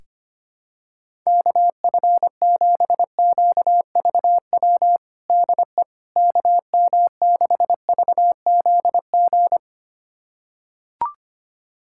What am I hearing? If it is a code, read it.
Morse code, heard as a keyed tone, about 25 wpm, 700 Hz. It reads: KF7QVW DE KM6VZG
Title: USA Callsign DE Callsign 25wpm